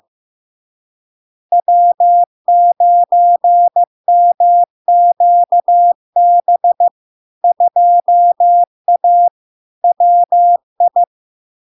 Morse code, heard as a keyed tone, about 15 wpm, 700 Hz. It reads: W9MQB 2A WI
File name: LARK-FD-2022-Log-in-CW-15wpm-700-Hz